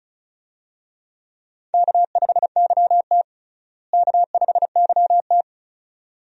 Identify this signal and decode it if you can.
Morse code, heard as a keyed tone, about 36 wpm, 700 Hz. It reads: K5YT K5YT
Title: USA Call Signs - 4 Characters - Repeated 2x - 35wpm